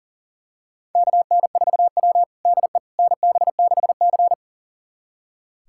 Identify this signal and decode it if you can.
Morse code, heard as a keyed tone, about 40 wpm, 690 Hz. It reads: KN4W DE NB6C
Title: USA Callsign DE Callsign - 4 Characters - 40wpm